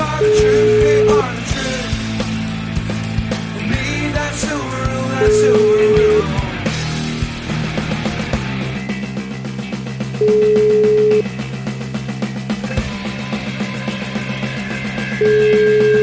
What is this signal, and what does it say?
Ringback tone (Europe, 425 Hz), with noise.